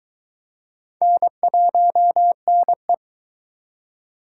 Morse code, heard as a keyed tone, about 21 wpm, 700 Hz. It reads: N1NE